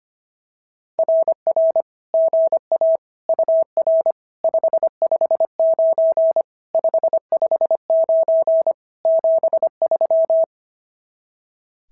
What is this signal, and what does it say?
Morse code, heard as a keyed tone, about 24 wpm, 650 Hz. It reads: RR GA UR 559 559 73
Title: SOTA QSO Practice 25wpm